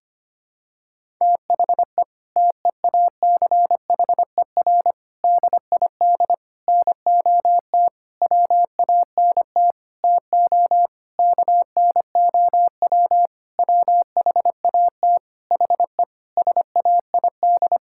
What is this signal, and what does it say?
Morse code, heard as a keyed tone, about 25 wpm, 700 Hz. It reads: THE TEACHER DID NOT WANT TO KNOW WHAT HE SAID
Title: Sentences from Top 1000 Words - Review Part 6 - 25wpm